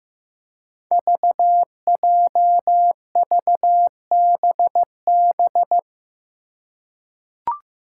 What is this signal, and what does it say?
Morse code, heard as a keyed tone, about 15 wpm, 700 Hz. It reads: VJVBB
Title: Five Challenging Letter-Number Combinations 15wpm